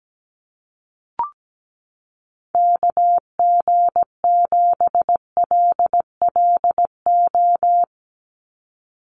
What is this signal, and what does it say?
Morse code, heard as a keyed tone, about 17 wpm, 700 Hz. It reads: KG7LLO
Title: USA Call Signs 17wpm